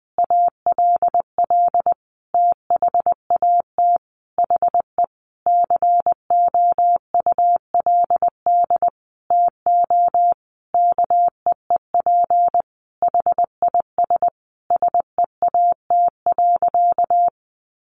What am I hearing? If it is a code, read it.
Morse code, heard as a keyed tone, about 20 wpm, 700 Hz. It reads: ALL THAT HE COULD TO KEEP HIS SEAT.
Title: Sentences from Top 1500 Words 20wpm